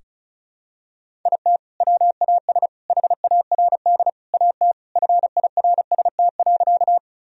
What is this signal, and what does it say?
Morse code, heard as a keyed tone, about 36 wpm, 700 Hz. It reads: IT WAS HARD AT FIRST.
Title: Sentences from Top 1000 Words - Review Part 1 - 35wpm